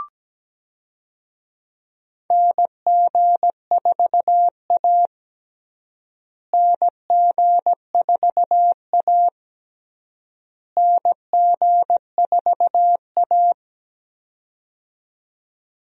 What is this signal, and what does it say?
Morse code, heard as a keyed tone, about 17 wpm, 700 Hz. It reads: NG4A NG4A NG4A
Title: USA Call Signs - 4 Characters - Repeated 3x - 17wpm